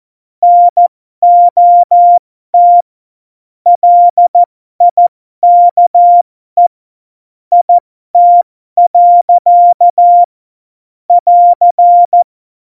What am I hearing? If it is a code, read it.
Morse code, heard as a keyed tone, about 14 wpm, spaced as for 12 wpm, 700 Hz. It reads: NOT LIKE IT. <AR>
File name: Morsekurs / Witze